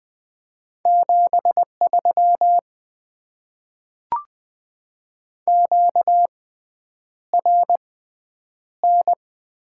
Morse code, heard as a keyed tone, about 20 wpm, spaced as for 9 wpm, 700 Hz. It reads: 73 QRN
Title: Top 90 QSO Elements - 6x Character Spacing - 20wpm